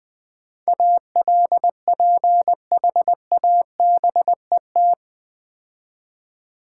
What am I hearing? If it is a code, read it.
Morse code, heard as a keyed tone, about 20 wpm, 700 Hz. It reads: ALPHABET
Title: Easy Words - Speed-Racing with Hints - 20wpm